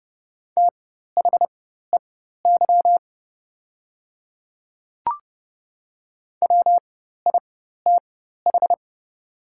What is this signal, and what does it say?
Morse code, heard as a keyed tone, about 30 wpm, spaced as for 14 wpm, 690 Hz. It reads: THEY WITH
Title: 4x Character Spacing - Top 100 Words - 30wpm